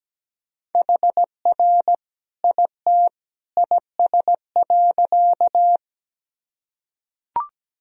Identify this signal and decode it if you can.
Morse code, heard as a keyed tone, about 17 wpm, 700 Hz. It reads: HR IT IS.
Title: Sentences from Top 200 Words and Abbreviations 17wpm